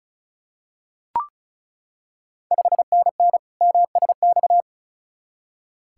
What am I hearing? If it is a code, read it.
Morse code, heard as a keyed tone, about 36 wpm, 690 Hz. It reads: 5NN MSX
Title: Ontario QSO Party Exchange 35wpm